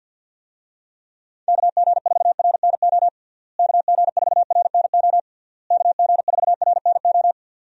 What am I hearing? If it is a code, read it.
Morse code, heard as a keyed tone, about 51 wpm, 700 Hz. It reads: KG4RNO KG4RNO KG4RNO
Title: USA Call Signs - 6 Characters - Repeated 3x - 50wpm